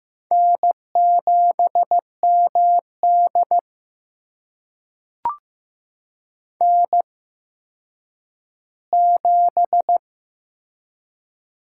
Morse code, heard as a keyed tone, about 15 wpm, spaced as for 9 wpm, 700 Hz. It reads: N7MD N7
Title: USA Call Signs - 4 Characters - 8x Character Spacing - 15wpm